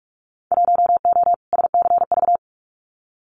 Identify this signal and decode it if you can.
Morse code, heard as a keyed tone, about 44 wpm, 700 Hz. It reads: 1O SCV
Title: Winter Field Day Exchanges 45wpm